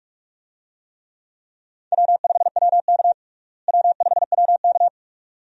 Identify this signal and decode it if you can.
Morse code, heard as a keyed tone, about 41 wpm, 700 Hz. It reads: W5WK W5WK
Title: USA Call Signs - 4 Characters - Repeated 2x - 45wpm